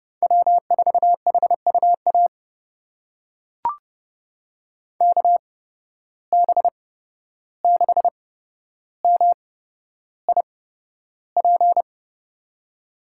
Morse code, heard as a keyed tone, about 30 wpm, spaced as for 12 wpm, 700 Hz. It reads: W4HUA KB6MIP
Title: USA Call Signs - 8x Character Spacing - 30wpm